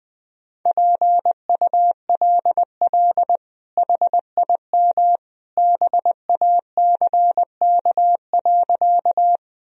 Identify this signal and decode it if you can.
Morse code, heard as a keyed tone, about 20 wpm, 700 Hz. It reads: PULL HIM BACK.
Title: Sentences from Top 1000 Words - Review Part 2 - 20wpm